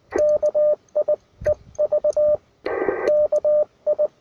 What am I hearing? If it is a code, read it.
Morse code, heard as a keyed tone, about 19 wpm, 600 Hz, with radio noise. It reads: KI EV KI
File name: attachments / UR4MCK IZ5GVP